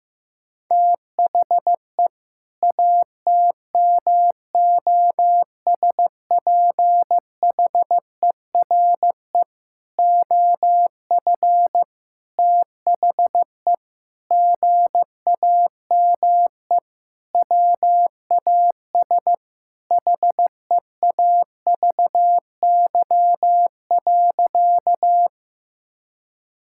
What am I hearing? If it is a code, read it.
Morse code, heard as a keyed tone, about 14 wpm, 700 Hz. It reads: THE ATMOSPHERE OF THE GAME WAS HEAVY.
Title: Sentences from Top 1600 Words 15wpm